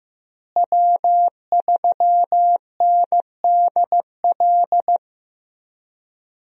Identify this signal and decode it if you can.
Morse code, heard as a keyed tone, about 15 wpm, 700 Hz. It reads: W3NDL
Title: USA Call Signs - 5 Characters - Repeated 3x - 15wpm